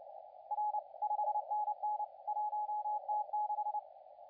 Morse code, heard as a keyed tone, about 29 wpm, 840 Hz, with radio noise. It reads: R 5NN 1T6
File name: la4c / sac-cw-2017 / CONTEST AUDIO 2100Z